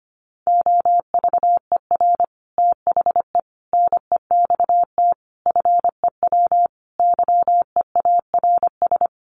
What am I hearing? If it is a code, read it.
Morse code, heard as a keyed tone, about 25 wpm, 700 Hz. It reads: OVER THE NEXT FEW YEARS
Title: Sets of 5 Words Encore 25wpm